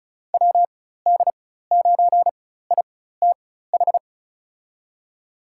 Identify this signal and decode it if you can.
Morse code, heard as a keyed tone, about 36 wpm, spaced as for 16 wpm, 700 Hz. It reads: WD9ITH
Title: USA Call Signs - 4x Character Spacing - 35wpm